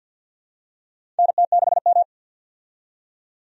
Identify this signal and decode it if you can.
Morse code, heard as a keyed tone, about 52 wpm, 700 Hz. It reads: NT6M
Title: USA Call Signs - 4 Characters - 50wpm